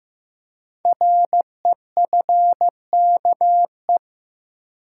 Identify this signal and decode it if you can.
Morse code, heard as a keyed tone, about 15 wpm, 700 Hz. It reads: REFKE
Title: Code Groups - 5 Letter:Numbers - 15wpm